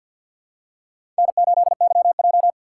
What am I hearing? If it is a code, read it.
Morse code, heard as a keyed tone, about 53 wpm, 700 Hz. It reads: N8YJ
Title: USA Call Signs - 3x Character Spacing - 50wpm